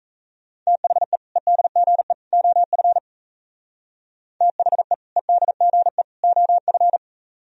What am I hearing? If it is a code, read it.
Morse code, heard as a keyed tone, about 34 wpm, 700 Hz. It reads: THE EDGE OF THE EDGE OF
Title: Speed-Racing - Sets of 3 Words 28wpm